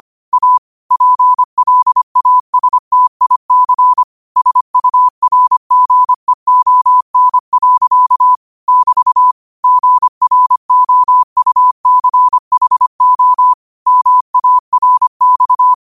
Morse code, heard as a keyed tone, about 25 wpm, 1000 Hz. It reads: A PLASTIC SURGEON. <BT> GROUCHO MARX
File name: qotdfiles / 25wpm